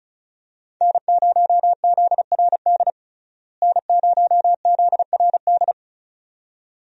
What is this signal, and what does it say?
Morse code, heard as a keyed tone, about 36 wpm, 700 Hz. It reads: N0ZRD N0ZRD
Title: USA Call Signs - 5 Characters - Repeated 2x - 35wpm